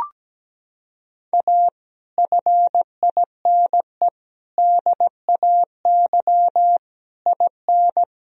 Morse code, heard as a keyed tone, about 17 wpm, 700 Hz. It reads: A FINE DAY IN
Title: Sentences from Top 1000 Words - Review Part 4 - 17wpm